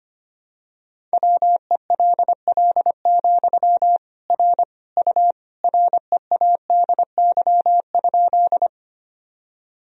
Morse code, heard as a keyed tone, about 25 wpm, 700 Hz. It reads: WELL, R U READY?
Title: Sentences from Top 400 Words and Abbreviations 25wpm